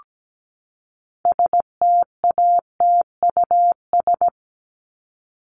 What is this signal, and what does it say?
Morse code, heard as a keyed tone, about 17 wpm, 700 Hz. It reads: STATUS
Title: Top 1251-1500 Words 17wpm